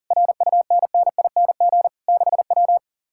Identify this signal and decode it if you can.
Morse code, heard as a keyed tone, about 39 wpm, 700 Hz. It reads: RUNNING 6W
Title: Speed-Racing Common QSO Phrases - 40wpm